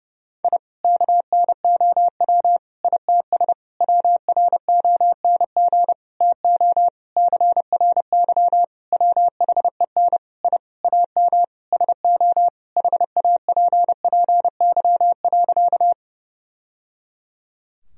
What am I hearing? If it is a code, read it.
Morse code, heard as a keyed tone, about 30 wpm, 700 Hz. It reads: I KNOW ITS WRONG TO CRY WHEN I AM SO HAPPY.
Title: Sentences from Top 1000 Words - Review Part 2 - 30wpm